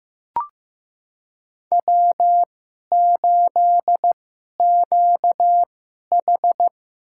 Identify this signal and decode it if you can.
Morse code, heard as a keyed tone, about 15 wpm, spaced as for 11 wpm, 700 Hz. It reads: W8QH
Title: USA Call Signs - 4 Characters - 2x Character Spacing - 15wpm